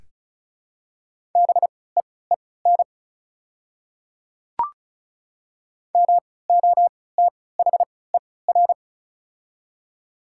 Morse code, heard as a keyed tone, about 36 wpm, spaced as for 20 wpm, 700 Hz. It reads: BEEN MOTHER
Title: 3x Character Spacing - Top 200 Words - 35wpm